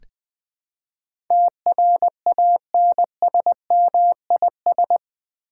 Morse code, heard as a keyed tone, about 20 wpm, 700 Hz. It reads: TRANSMIS
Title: Build A Word - 4 Letters At A Time - 12-Letter Words - 20wpm